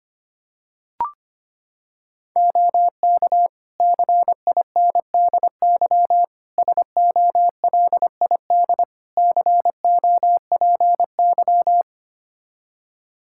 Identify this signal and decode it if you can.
Morse code, heard as a keyed tone, about 25 wpm, 700 Hz. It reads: OK CINDY SOLID COPY
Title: K1USN SST QSO Practice 25wpm